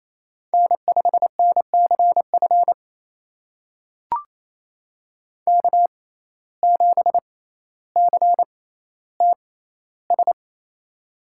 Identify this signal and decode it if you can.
Morse code, heard as a keyed tone, about 28 wpm, spaced as for 14 wpm, 700 Hz. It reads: N5NCF K7CTS
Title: USA Call Signs - 5 Characters - 6x Character Spacing - 28wpm